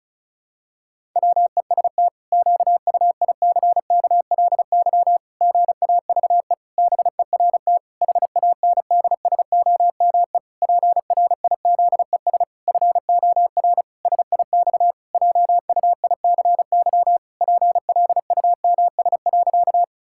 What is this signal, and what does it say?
Morse code, heard as a keyed tone, about 36 wpm, 700 Hz. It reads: WEST QUICKLY GAVE BERT HANDSOME PRIZES FOR SIX JUICY PLUMS.
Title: Pangram Challenge v2 - 35wpm